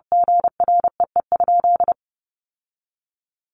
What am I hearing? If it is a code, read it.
Morse code, heard as a keyed tone, about 30 wpm, 700 Hz. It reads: GREE?
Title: Sentences from Top 1100 Words 30wpm